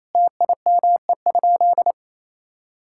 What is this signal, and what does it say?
Morse code, heard as a keyed tone, about 28 wpm, 700 Hz. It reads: TIME?
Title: Sentences from Top 1000 Words - Review Part 5 - 28wpm